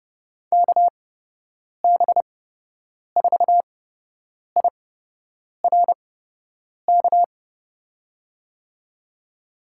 Morse code, heard as a keyed tone, about 30 wpm, spaced as for 8 wpm, 700 Hz. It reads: KB4IRK
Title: USA Call Signs - 8x Character Spacing - 30wpm